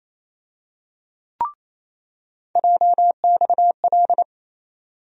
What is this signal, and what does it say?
Morse code, heard as a keyed tone, about 28 wpm, 700 Hz. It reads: JXL
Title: USA Call Sign Postfixes - 3 Letters 28wpm